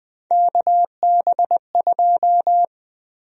Morse code, heard as a keyed tone, about 20 wpm, 700 Hz. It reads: KB2
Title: USA Call Sign Prefixes - 20wpm